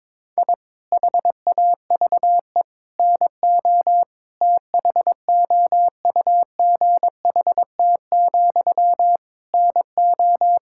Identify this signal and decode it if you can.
Morse code, heard as a keyed tone, about 22 wpm, 700 Hz. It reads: I HAVE NO THOUGHT, NO
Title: Sentences from Top 1000 Words - Review Part 6 - 22wpm